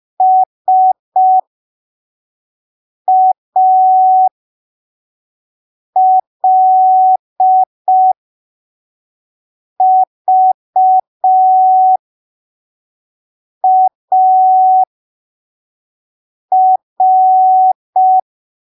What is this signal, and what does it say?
Morse code, heard as a keyed tone, about 5 wpm, spaced as for 3 wpm, 750 Hz. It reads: SALVAR
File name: arquivos / texto / texto 03 5ppm